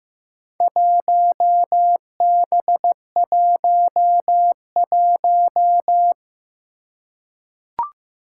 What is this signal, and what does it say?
Morse code, heard as a keyed tone, about 15 wpm, 700 Hz. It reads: 1B11
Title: Four Challenging Letter-Number Combinations 15wpm